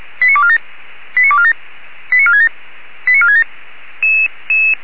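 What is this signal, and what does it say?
Five-tone selective call (ZVEI), heard as quick runs of five tones, tones of about 70 ms. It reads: 89247 89247 89467 89467
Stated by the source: Five-tone ZVEI selective call 89297 received on an SDR.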